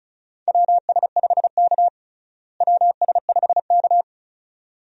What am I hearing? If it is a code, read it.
Morse code, heard as a keyed tone, about 34 wpm, 700 Hz. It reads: WS5K WS5K
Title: USA Call Signs - 4 Characters - Repeated 2x - 35wpm